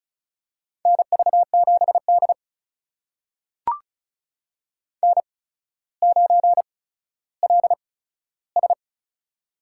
Morse code, heard as a keyed tone, about 36 wpm, spaced as for 15 wpm, 700 Hz. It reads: NV7D N9LS
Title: USA Call Signs - 4 Characters - 8x Character Spacing - 35wpm